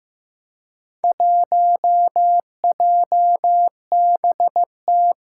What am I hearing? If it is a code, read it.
Morse code, heard as a keyed tone, about 15 wpm, 700 Hz. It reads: 1JBT
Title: Four Challenging Letter-Number Combinations 15wpm